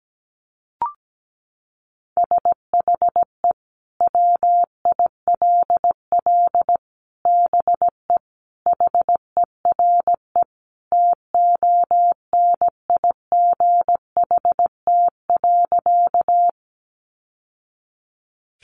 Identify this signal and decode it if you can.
Morse code, heard as a keyed tone, about 17 wpm, 700 Hz. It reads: SHE WILL BE HERE TONIGHT.
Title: Sentences from Top 600 Words 17wpm